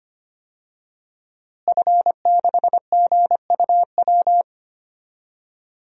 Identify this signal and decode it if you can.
Morse code, heard as a keyed tone, about 25 wpm, 700 Hz. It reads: F6GUW
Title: DXCC-100 Call Signs - 5 Characters - Repeated 2x - 25wpm